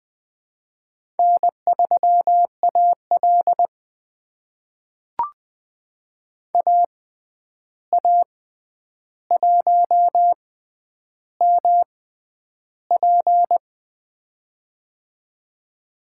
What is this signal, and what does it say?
Morse code, heard as a keyed tone, about 20 wpm, spaced as for 10 wpm, 700 Hz. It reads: N3AL AA1MP
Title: USA Call Signs - 6x Character Spacing - 20wpm